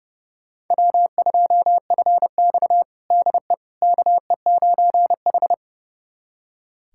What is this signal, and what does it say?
Morse code, heard as a keyed tone, about 30 wpm, 700 Hz. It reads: W2FX DE KE9H
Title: USA Callsign DE Callsign - 4 Characters - 30wpm